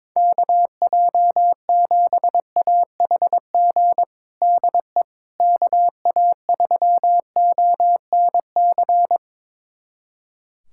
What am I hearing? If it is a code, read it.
Morse code, heard as a keyed tone, about 22 wpm, 700 Hz. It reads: KJ7AHG DE KA3ONC
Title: USA Callsign DE Callsign 22wpm